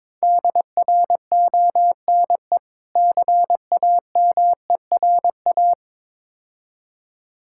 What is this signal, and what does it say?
Morse code, heard as a keyed tone, about 22 wpm, 690 Hz. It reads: DRONE CAMERA
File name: Difficult Words with Hints - 22wpm